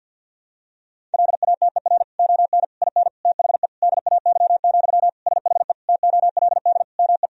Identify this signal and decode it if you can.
Morse code, heard as a keyed tone, about 50 wpm, 700 Hz. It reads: LATER ON IN THE DAY, SHE TOLD ME
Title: Sentences from Top 1000 Words - Review Part 4 - 50wpm